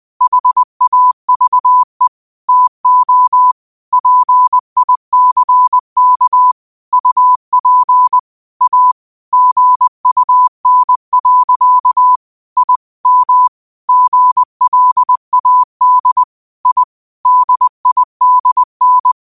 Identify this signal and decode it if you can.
Morse code, heard as a keyed tone, about 20 wpm, 1000 Hz. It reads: HAVE TO PICK UP A GUN. I M GLAD I DIDN
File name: qotdfiles / 20wpm